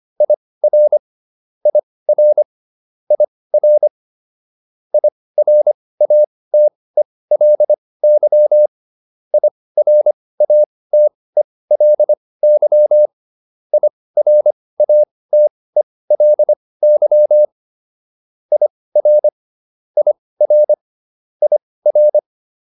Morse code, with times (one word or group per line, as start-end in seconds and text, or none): 0.2-1.0 IR
1.7-2.4 IR
3.1-3.9 IR
4.9-8.7 IRATELY
9.3-13.1 IRATELY
13.7-17.4 IRATELY
18.5-19.3 IR
20.0-20.7 IR
21.4-22.2 IR